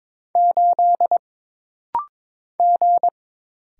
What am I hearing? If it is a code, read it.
Morse code, heard as a keyed tone, about 22 wpm, 700 Hz. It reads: 8 G
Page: This rapid-fire timing is geared to push your ICR (Instant Character Recognition) skills.